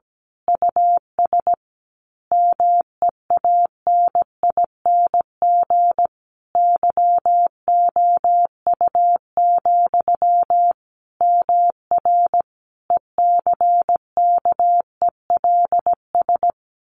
Morse code, heard as a keyed tone, about 17 wpm, 700 Hz. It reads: US MEANING YOU, MR ECKELS
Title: A Sound Of Thunder 17wpm